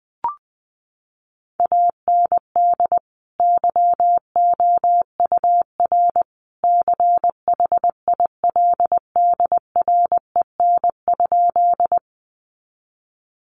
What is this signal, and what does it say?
Morse code, heard as a keyed tone, about 20 wpm, 700 Hz. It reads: AND YOUR CHILDREN?
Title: Sentences from Top 300 Words 20wpm